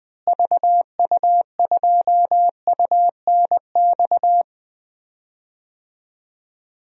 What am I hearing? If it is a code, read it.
Morse code, heard as a keyed tone, about 21 wpm, 700 Hz. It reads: VU2UNX
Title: DXCC-100 Call Signs - 6 Characters - Repeated 3x - 20wpm